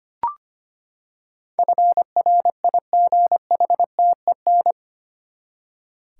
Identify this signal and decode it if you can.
Morse code, heard as a keyed tone, about 25 wpm, 700 Hz. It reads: FRIGHTEN
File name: Eight Letter Words - 25wpm